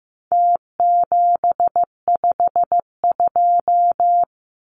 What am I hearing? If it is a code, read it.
Morse code, heard as a keyed tone, about 15 wpm, 700 Hz. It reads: T752
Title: Numbers - Contest Style - 4-Digit 15wpm